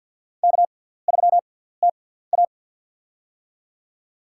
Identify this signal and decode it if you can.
Morse code, heard as a keyed tone, about 48 wpm, spaced as for 17 wpm, 700 Hz. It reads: K3TA
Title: USA Call Signs - 4 Characters - 6x Character Spacing - 50wpm